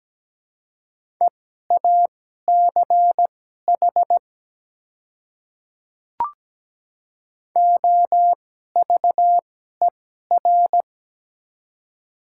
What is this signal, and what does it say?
Morse code, heard as a keyed tone, about 17 wpm, spaced as for 12 wpm, 700 Hz. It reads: EACH OVER
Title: 2x Character Spacing - Top 100 Words - 17wpm